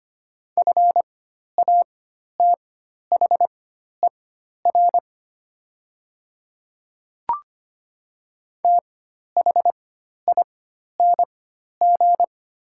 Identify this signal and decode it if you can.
Morse code, heard as a keyed tone, about 25 wpm, spaced as for 12 wpm, 700 Hz. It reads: FATHER THING
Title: 4x Character Spacing - Top 200 Words - 25wpm